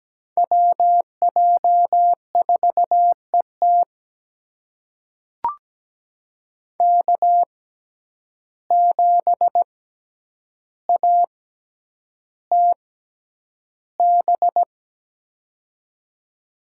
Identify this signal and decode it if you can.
Morse code, heard as a keyed tone, about 17 wpm, spaced as for 9 wpm, 700 Hz. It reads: WJ4ET K7ATB
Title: USA Call Signs - 5 Characters - 6x Character Spacing - 17wpm